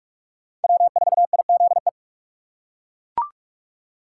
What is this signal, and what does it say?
Morse code, heard as a keyed tone, about 42 wpm, 700 Hz. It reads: W4IZE
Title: USA Call Signs - 5 Characters - 4x Character Spacing - 45wpm